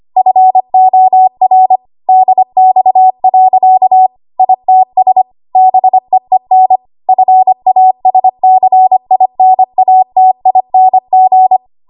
Morse code, heard as a keyed tone, about 25 wpm, 750 Hz. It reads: FOR DX. ITS BEEN FASCINATING
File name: Archive / 25 WPM / 25WPM